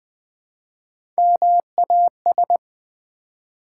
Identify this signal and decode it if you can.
Morse code, heard as a keyed tone, about 20 wpm, 700 Hz. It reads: MAS
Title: Build A Word - 1 Letter At A Time - 4-Letter Words - 20wpm